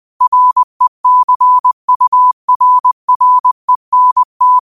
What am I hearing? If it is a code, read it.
Morse code, heard as a keyed tone, about 20 wpm, 1000 Hz. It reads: RECURRENT
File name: qotdfiles / 20wpm